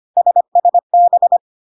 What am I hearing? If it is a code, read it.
Morse code, heard as a keyed tone, about 25 wpm, 700 Hz. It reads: SSB
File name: CW abbreviations alphabetical repeated 3 times 25 25wpm